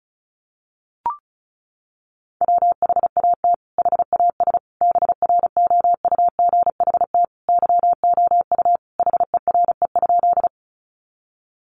Morse code, heard as a keyed tone, about 34 wpm, 700 Hz. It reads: WHAT HAS BROUGHT YOU HERE?
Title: Sentences from Top 500 Words 35wpm